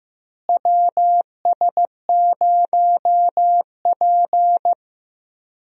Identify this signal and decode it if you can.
Morse code, heard as a keyed tone, about 15 wpm, 700 Hz. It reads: WS0P
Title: USA Call Signs - 4 Characters - Repeated 3x - 15wpm